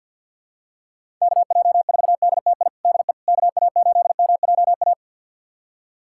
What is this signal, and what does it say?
Morse code, heard as a keyed tone, about 52 wpm, 700 Hz. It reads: KJ4DTI DE KA8MJA
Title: USA Callsign DE Callsign 50wpm